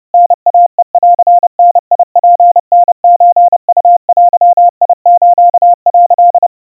Morse code (Japanese Wabun code), heard as a keyed tone, about 30 wpm, 700 Hz. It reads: タイヘンダツタソウデス」
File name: 110 cpm 700 Hz Crispy